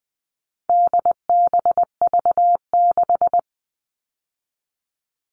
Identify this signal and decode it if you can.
Morse code, heard as a keyed tone, about 20 wpm, 700 Hz. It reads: DBV6
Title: All Cut Numbers - 4-Digit 20wpm